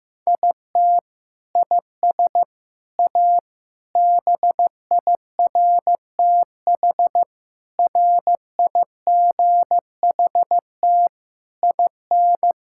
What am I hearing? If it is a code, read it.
Morse code, heard as a keyed tone, about 16 wpm, 700 Hz. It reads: IT IS A BIRTH RIGHT IN
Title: Sentences from Top 1000 Words - Review Part 5 - 15wpm